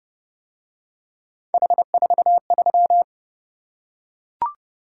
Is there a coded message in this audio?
Yes: Morse code from 1.5 to 3.0 s, heard as a keyed tone, about 27 wpm, 700 Hz. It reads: H43